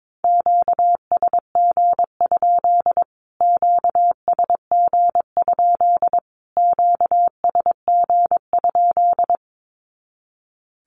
Morse code, heard as a keyed tone, about 22 wpm, 700 Hz. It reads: QSG? QSG? QSG?
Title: Q-Codes Repeated 3x 22wpm